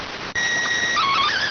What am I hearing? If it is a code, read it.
SSTV (slow-scan television), PD 290.